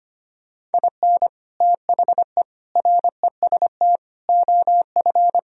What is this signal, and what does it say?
Morse code, heard as a keyed tone, about 32 wpm, 700 Hz. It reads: IN THE REST OF